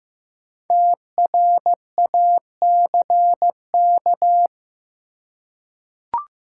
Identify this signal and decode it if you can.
Morse code, heard as a keyed tone, about 15 wpm, 700 Hz. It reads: TRACK
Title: Five Letter Words - 15wpm